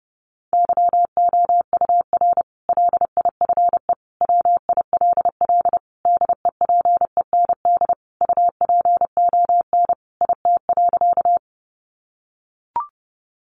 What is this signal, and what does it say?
Morse code, heard as a keyed tone, about 28 wpm, 700 Hz. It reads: YOUR LIFE WILL DEPEND UPON IT.
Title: Sentences from Top 1300 Words 30wpm